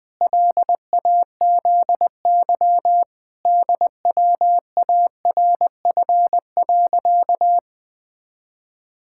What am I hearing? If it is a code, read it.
Morse code, heard as a keyed tone, about 19 wpm, 700 Hz. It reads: LAZY DWARF.
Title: Pangram Challenge v2 - 20wpm